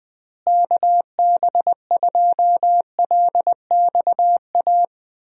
Morse code, heard as a keyed tone, about 20 wpm, 700 Hz. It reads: KB2LXA
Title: USA Call Signs - 6 Characters - Repeated 3x - 20wpm